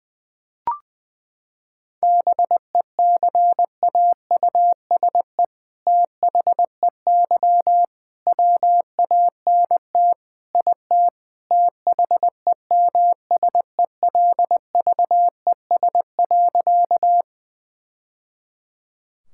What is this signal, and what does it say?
Morse code, heard as a keyed tone, about 20 wpm, 700 Hz. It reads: BECAUSE THEY WANT IT THEMSELVES.
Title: Sentences from Top 1000 Words - Review Part 5 - 20wpm